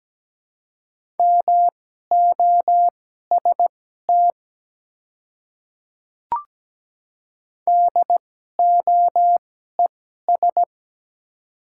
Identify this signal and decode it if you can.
Morse code, heard as a keyed tone, about 17 wpm, spaced as for 12 wpm, 700 Hz. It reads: MOST DOES
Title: 2x Character Spacing - Top 200 Words - 17wpm